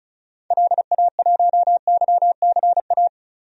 Morse code, heard as a keyed tone, about 36 wpm, 700 Hz. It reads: LA1YCA
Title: DXCC-100 Call Signs - 6 Characters - 35wpm